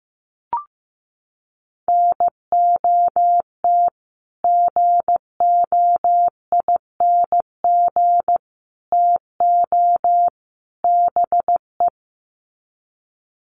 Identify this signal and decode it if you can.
Morse code, heard as a keyed tone, about 15 wpm, 700 Hz. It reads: NOT GOING TO BE
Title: Sets of 4 Words 15wpm